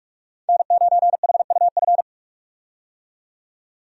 Morse code, heard as a keyed tone, about 42 wpm, 700 Hz. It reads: N9HUF
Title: USA Call Signs - 5 Characters - 45wpm